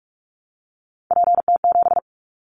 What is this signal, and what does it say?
Morse code, heard as a keyed tone, about 46 wpm, 700 Hz. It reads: PT7
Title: Callsign Prefixes for Top 100 DXCC - 45wpm